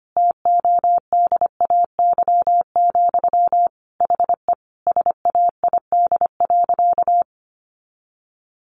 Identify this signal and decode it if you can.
Morse code, heard as a keyed tone, about 25 wpm, 700 Hz. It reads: TODAY, HE SAID.
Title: Sentences from Top 600 Words 25wpm